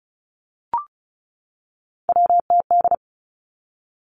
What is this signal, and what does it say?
Morse code, heard as a keyed tone, about 36 wpm, 700 Hz. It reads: WTD
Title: Three Challenging Letter-Number Combinations 35wpm